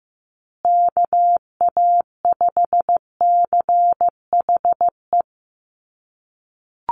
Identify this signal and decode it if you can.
Morse code, heard as a keyed tone, about 15 wpm, 700 Hz. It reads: KA5CHE
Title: USA Call Signs Repeated 3x 15wpm